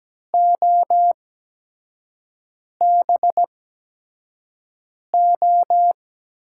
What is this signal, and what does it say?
Morse code, heard as a keyed tone, about 17 wpm, spaced as for 5 wpm, 700 Hz. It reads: OBO
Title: USA Call Signs - 6 Characters - 8x Character Spacing - 17wpm